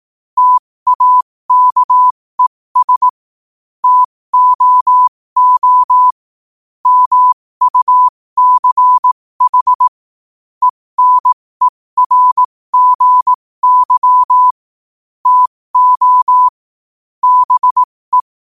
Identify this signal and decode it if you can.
Morse code, heard as a keyed tone, about 18 wpm, spaced as for 15 wpm, 1000 Hz. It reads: TAKES TOO MUCH ENERGY TO BE
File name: qotdfiles / 15wpm